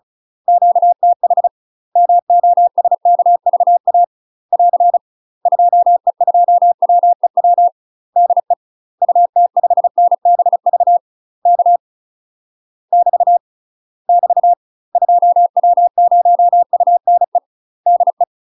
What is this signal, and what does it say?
Morse code, heard as a keyed tone, about 34 wpm, 700 Hz. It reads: QTH MOSKVA <AR> 2E2WEW DE UT5NBV K <BT> <BT> 2W0UNE DE
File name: UK MASTER QSOs TRAINING / 35WPM / UK MASTER QSOs 0031